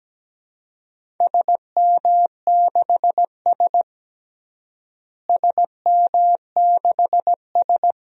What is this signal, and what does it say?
Morse code, heard as a keyed tone, about 17 wpm, 700 Hz. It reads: SM6S SM6S
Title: DXCC-100 Call Signs - 4 Characters - Repeated 2x - 17wpm